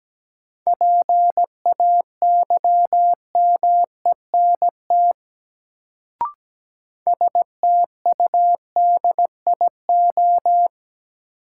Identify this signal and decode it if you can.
Morse code, heard as a keyed tone, about 17 wpm, 700 Hz. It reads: PAYMENT STUDIO
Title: Sentences from Top 1200 Words - New Word Drill Alt - 17wpm